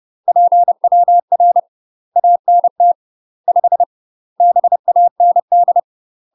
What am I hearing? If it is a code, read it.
Morse code, heard as a keyed tone, about 30 wpm, 700 Hz. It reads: PWR ANT 5 BAND
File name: UK MASTER QSOs TRAINING/30WPM